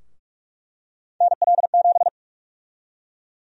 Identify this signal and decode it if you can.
Morse code, heard as a keyed tone, about 46 wpm, 700 Hz. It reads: NL7
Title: USA Call Sign Prefixes - 45wpm